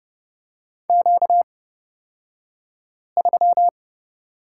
Morse code, heard as a keyed tone, about 30 wpm, spaced as for 5 wpm, 700 Hz. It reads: Q3
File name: Lesson 38d - Letter q - Two Characters 30wpm